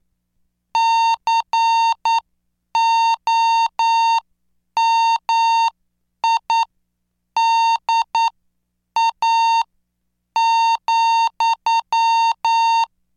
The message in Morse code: COMIDA,